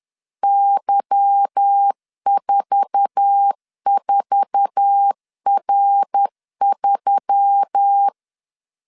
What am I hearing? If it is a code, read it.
Morse code, heard as a keyed tone, about 10 wpm, 800 Hz. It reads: Y44R3
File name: CW / DET / 1 9 1500C S B 10wpm 4439